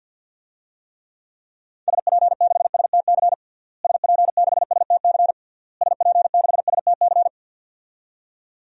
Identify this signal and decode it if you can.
Morse code, heard as a keyed tone, about 51 wpm, 700 Hz. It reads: SP6STC SP6STC SP6STC
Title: DXCC-100 Call Signs - 6 Characters - Repeated 3x - 50wpm